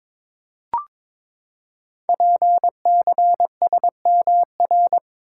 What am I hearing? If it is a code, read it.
Morse code, heard as a keyed tone, about 22 wpm, 700 Hz. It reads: PCSMR
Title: Code Groups - 5 Letter:Numbers - 22wpm